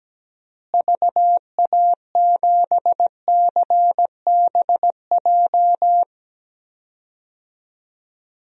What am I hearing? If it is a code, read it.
Morse code, heard as a keyed tone, about 17 wpm, 700 Hz. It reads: VA7CBJ
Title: DXCC-100 Call Signs - 6 Characters - Repeated 3x - 17wpm